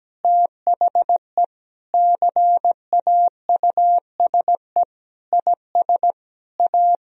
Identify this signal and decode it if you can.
Morse code, heard as a keyed tone, about 17 wpm, 700 Hz. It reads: THE CAUSE IS A
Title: Sentences from Top 1000 Words - Review Part 1 - 17wpm